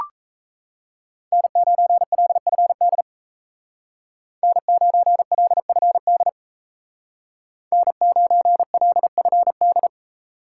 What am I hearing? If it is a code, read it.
Morse code, heard as a keyed tone, about 35 wpm, 700 Hz. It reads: N9LFD N9LFD N9LFD
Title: Speed Racing - USA 1x3 Call Signs 28wpm